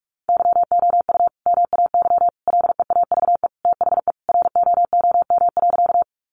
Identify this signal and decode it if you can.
Morse code, heard as a keyed tone, about 44 wpm, 700 Hz. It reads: YOU MAY LEAVE THE ROOM.
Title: Sentences from Top 400 Words 45wpm